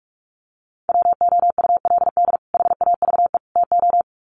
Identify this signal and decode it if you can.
Morse code, heard as a keyed tone, about 44 wpm, 700 Hz. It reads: WOULD HAVE TO
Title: Sets of 3 Words 45wpm